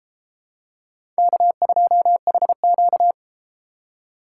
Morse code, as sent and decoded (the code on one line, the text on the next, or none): -.- ..--- .... --.-
K2HQ